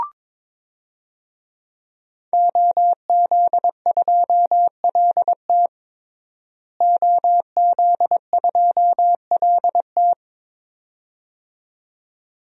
Morse code, heard as a keyed tone, about 22 wpm, 700 Hz. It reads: OZ2LT OZ2LT
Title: DXCC-100 Call Signs - 5 Characters - Repeated 2x - 22wpm